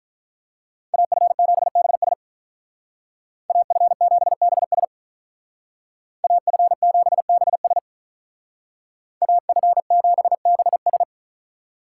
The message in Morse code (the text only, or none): AF7BS AF7BS AF7BS AF7BS